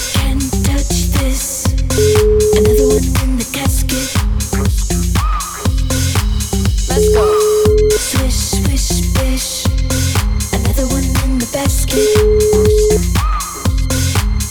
Ringback tone (Europe, 425 Hz), with noise.